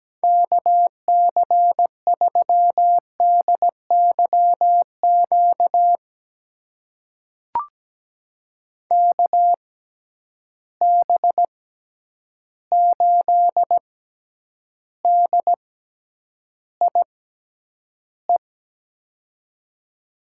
Morse code, heard as a keyed tone, about 17 wpm, spaced as for 9 wpm, 700 Hz. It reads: KC3DYQ KB8DIE
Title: USA Call Signs - 6x Character Spacing - 17wpm